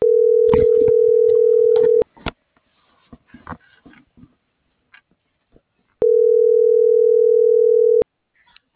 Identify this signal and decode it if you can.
Ringback tone (North America), 440 + 480 Hz.